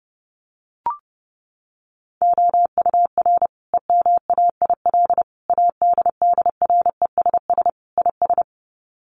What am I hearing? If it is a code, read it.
Morse code, heard as a keyed tone, about 30 wpm, 700 Hz. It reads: OUR EMAIL ADDRESS IS
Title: Sets of 5 Words Encore2 30wpm